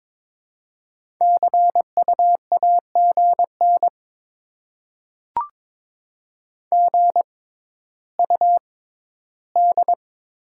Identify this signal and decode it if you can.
Morse code, heard as a keyed tone, about 22 wpm, spaced as for 13 wpm, 700 Hz. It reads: CUAGN GUD
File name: Top 20 QSO Elements - 6x Character Spacing - 22wpm